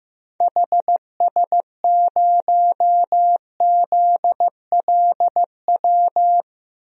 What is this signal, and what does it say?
Morse code, heard as a keyed tone, about 15 wpm, 700 Hz. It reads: HS0ZLW
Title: DXCC-100 Call Signs - 6 Characters - Repeated 2x - 15wpm